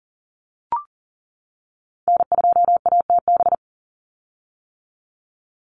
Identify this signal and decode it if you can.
Morse code, heard as a keyed tone, about 40 wpm, 700 Hz. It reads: N2ATB
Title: Call Signs - 40wpm